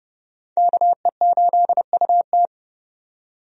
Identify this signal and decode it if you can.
Morse code, heard as a keyed tone, about 30 wpm, 700 Hz. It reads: KE8UT
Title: USA Call Signs - 5 Characters - 30wpm